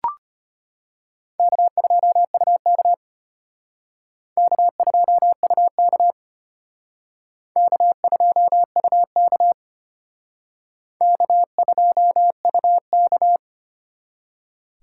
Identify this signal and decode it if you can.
Morse code, heard as a keyed tone, about 31 wpm, 700 Hz. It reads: K2UK K2UK K2UK K2UK
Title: Speed Racing - USA 1x2 Call Signs 25wpm